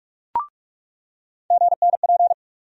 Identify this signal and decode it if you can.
Morse code, heard as a keyed tone, about 45 wpm, 700 Hz. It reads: GNP
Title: USA Call Sign Postfixes - 3 Letters 45wpm